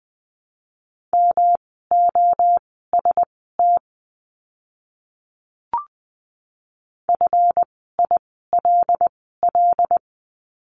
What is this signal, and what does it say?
Morse code, heard as a keyed tone, about 20 wpm, spaced as for 14 wpm, 700 Hz. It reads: MOST FILL
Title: 139 Lesson 23f - Letter f - Words 20wpm